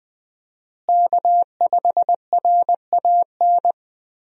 Morse code, heard as a keyed tone, about 20 wpm, 700 Hz. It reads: K5RAN
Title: USA Call Signs - 5 Characters - Repeated 3x - 20wpm